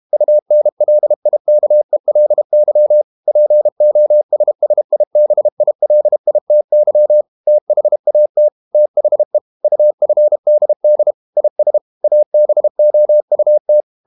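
Morse code, heard as a keyed tone, about 32 wpm, 600 Hz. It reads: UNLIKELY POSSIBILITY THAT THE UFDD IS ABOUT
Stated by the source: Morse code practice at 32 wpm from CW med Gnister, reading Ken Follett's novel Never in English.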